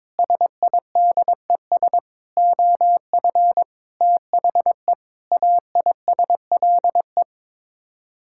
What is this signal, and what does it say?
Morse code, heard as a keyed tone, about 22 wpm, 700 Hz. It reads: SIDES OF THE AISLE
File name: practice / 22wpm / Sets of 6 Words - 22wpm